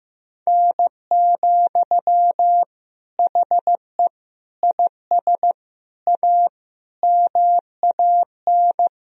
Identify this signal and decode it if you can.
Morse code, heard as a keyed tone, about 15 wpm, 700 Hz. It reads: N, HE IS A MAN
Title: Sentences from Top 400 Words and Abbreviations 15wpm